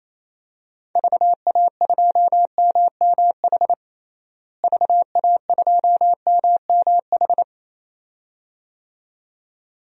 Morse code, heard as a keyed tone, about 28 wpm, 700 Hz. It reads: VA2MMH VA2MMH
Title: DXCC-100 Call Signs - 6 Characters - Repeated 2x - 28wpm